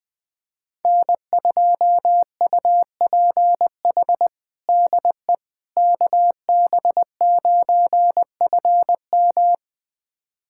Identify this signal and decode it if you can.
Morse code, heard as a keyed tone, about 20 wpm, 700 Hz. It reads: N2UPH DE KB9FM
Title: USA Callsign DE Callsign - 5 Characters - 20wpm